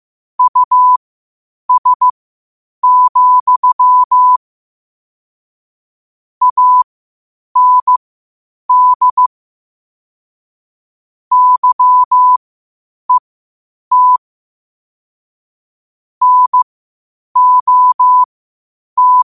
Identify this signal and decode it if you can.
Morse code, heard as a keyed tone, about 15 wpm, spaced as for 8 wpm, 1000 Hz. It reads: US, AND YET NOT
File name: qotdfiles / 8wpm